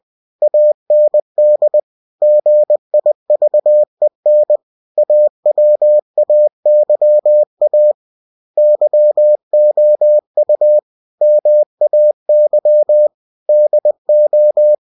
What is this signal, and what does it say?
Morse code, heard as a keyed tone, about 20 wpm, 600 Hz. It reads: AND GIVEN AWAYA YOU MAY DO
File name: cw-books / winnie-the-pooh / winnie-20wpm-page-0112